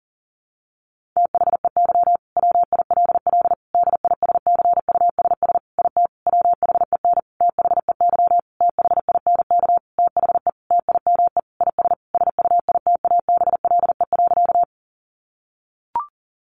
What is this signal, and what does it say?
Morse code, heard as a keyed tone, about 40 wpm, 700 Hz. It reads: THEY WILL DISCUSS IT WHEN THEY THINK THE TIME IS SUITABLE.
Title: Sentences from Top 1400 Words 40wpm